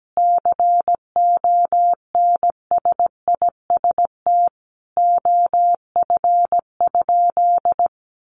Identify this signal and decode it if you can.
Morse code, heard as a keyed tone, about 17 wpm, 700 Hz. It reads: CONSIST OF?
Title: Sentences from Top 1400 Words 17wpm